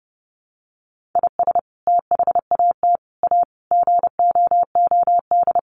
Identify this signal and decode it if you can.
Morse code, heard as a keyed tone, about 30 wpm, 700 Hz. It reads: IS THAT A GOOD
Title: Sentences from Top 1500 Words 30wpm